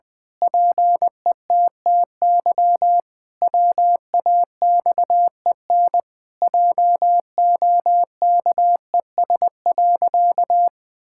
Morse code, heard as a keyed tone, about 20 wpm, 700 Hz. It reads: PETTY WAXEN JOKES.
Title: Pangram Challenge v2 - 20wpm